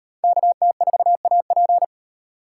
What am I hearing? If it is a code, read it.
Morse code, heard as a keyed tone, about 38 wpm, 700 Hz. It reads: KT4AP